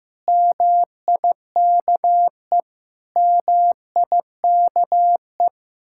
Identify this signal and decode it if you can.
Morse code, heard as a keyed tone, about 15 wpm, 700 Hz. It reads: MIKE MIKE
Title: Speed-Racing Common QSO Phrases Encore - 15wpm